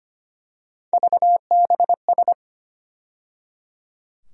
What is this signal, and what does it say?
Morse code, heard as a keyed tone, about 25 wpm, 700 Hz. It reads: VBS
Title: USA Call Sign Postfixes - 3 Letters 25wpm